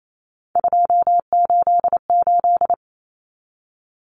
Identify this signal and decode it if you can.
Morse code, heard as a keyed tone, about 28 wpm, 700 Hz. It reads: 288
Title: Common Cut Numbers - 3-Digit 28wpm